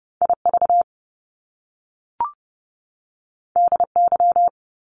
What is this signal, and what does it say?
Morse code, heard as a keyed tone, about 30 wpm, 700 Hz. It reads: IV DY
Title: Top Two Letter Combinations in Words 30wpm